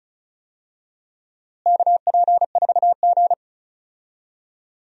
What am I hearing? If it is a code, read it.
Morse code, heard as a keyed tone, about 36 wpm, 700 Hz. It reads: KP4G
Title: DXCC-100 Call Signs - 4 Characters - Repeated 2x - 35wpm